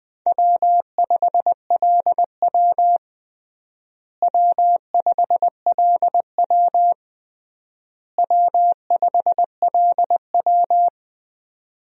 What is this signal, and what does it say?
Morse code, heard as a keyed tone, about 20 wpm, 700 Hz. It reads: W5LW W5LW W5LW
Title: USA Call Signs - 4 Characters - Repeated 3x - 20wpm